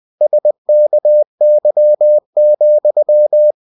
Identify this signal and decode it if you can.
Morse code, heard as a keyed tone, about 20 wpm, 600 Hz. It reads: SKY,
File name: winnie-20wpm